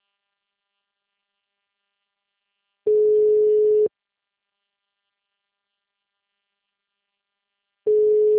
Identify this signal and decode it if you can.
Telephone call progress tone (Europe, 425 Hz), heard as a telephone network's tone, about 420 Hz. Ringback tone.